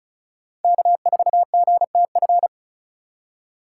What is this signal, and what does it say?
Morse code, heard as a keyed tone, about 36 wpm, 700 Hz. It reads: K4GTF
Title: USA Call Signs - 3x Character Spacing - 35wpm